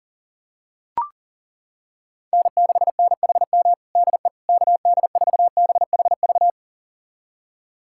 Morse code, heard as a keyed tone, about 40 wpm, 700 Hz. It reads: N6NHM DE KD4BHV
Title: USA Callsign DE Callsign 40wpm